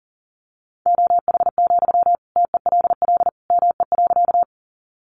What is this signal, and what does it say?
Morse code, heard as a keyed tone, about 40 wpm, 700 Hz. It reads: OH, TELL ME.
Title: Sentences from Top 500 Words 40wpm